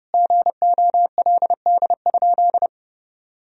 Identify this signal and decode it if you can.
Morse code, heard as a keyed tone, about 30 wpm, 700 Hz. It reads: GOLD?
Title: Sentences from Top 1000 Words - Review Part 3 - 30wpm